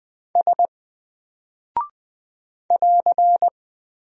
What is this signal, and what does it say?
Morse code, heard as a keyed tone, about 20 wpm, 700 Hz. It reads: S <AR>
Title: Single Character (Letter:Number:Punctuation:Prosign) 20wpm